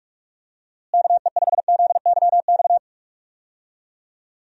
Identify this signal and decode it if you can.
Morse code, heard as a keyed tone, about 45 wpm, 700 Hz. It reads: KE5ZYX
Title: USA Call Signs - 6 Characters - Repeated 2x - 45wpm